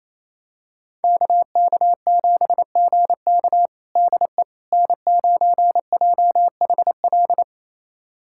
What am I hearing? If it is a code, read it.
Morse code, heard as a keyed tone, about 28 wpm, 700 Hz. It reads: KK7GK DE N9JHL
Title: USA Callsign DE Callsign - 5 Characters - 28wpm